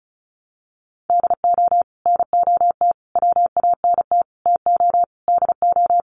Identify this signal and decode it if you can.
Morse code, heard as a keyed tone, about 34 wpm, 700 Hz. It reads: DO NOT WANT TO DO
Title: Sets of 5 Words 35wpm